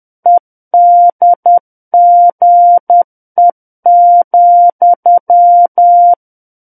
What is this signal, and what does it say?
Morse code, heard as a keyed tone, about 10 wpm, 700 Hz. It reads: EDGE,
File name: Book One / 01-Chapter 1 - 10 WPM